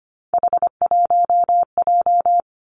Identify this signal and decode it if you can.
Morse code, heard as a keyed tone, about 25 wpm, 700 Hz. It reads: H1J
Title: Three Challenging Letter-Number Combinations 25wpm